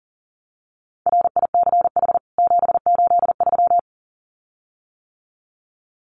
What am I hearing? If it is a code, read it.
Morse code, heard as a keyed tone, about 40 wpm, 700 Hz. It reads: RICH 783
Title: CWT Exchange 40wpm